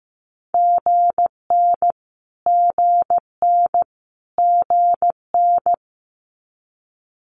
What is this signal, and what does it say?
Morse code, heard as a keyed tone, about 15 wpm, 700 Hz. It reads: GN GN GN
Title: Multi-letter Phonemes Repeated 3x 15wpm